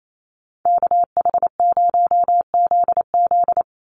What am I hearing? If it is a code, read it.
Morse code, heard as a keyed tone, about 28 wpm, 700 Hz. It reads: KH0ZZ
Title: Call Signs - 28wpm